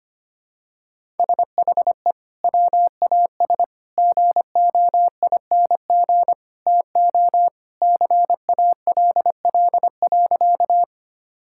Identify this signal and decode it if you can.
Morse code, heard as a keyed tone, about 25 wpm, 700 Hz. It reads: SHE WAS GOING TO CALL.
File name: practice / 25wpm / Sentences with 3-Word Phrases - 25wpm